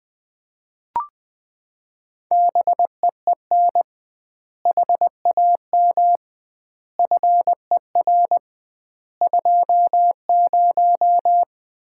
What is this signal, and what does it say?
Morse code, heard as a keyed tone, about 20 wpm, 700 Hz. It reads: BEEN HAM FER 20
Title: Common QSO Phrases - 2x Word-Spacing - 20wpm